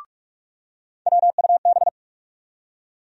Morse code, heard as a keyed tone, about 46 wpm, 700 Hz. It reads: WUB
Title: USA Call Sign Postfixes - 3 Letters 45wpm